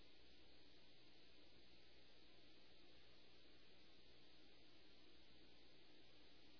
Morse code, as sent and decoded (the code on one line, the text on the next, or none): none